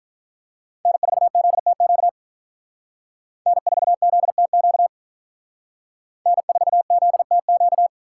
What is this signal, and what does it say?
Morse code, heard as a keyed tone, about 45 wpm, 700 Hz. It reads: N4ZTQ N4ZTQ N4ZTQ